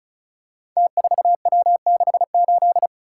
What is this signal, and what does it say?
Morse code, heard as a keyed tone, about 34 wpm, 700 Hz. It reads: T4W68
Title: Code Groups - 5 Letter:Numbers - 35wpm